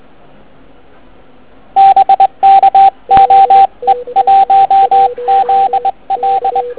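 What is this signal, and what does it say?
Morse code, heard as a keyed tone, about 20 wpm, 780 Hz, with radio noise. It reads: BK OE1ZL
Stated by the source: Morning 80m pile-up recorded at nearly the same time on three transceivers: FT1000MKV, FT817 and ATS-3B.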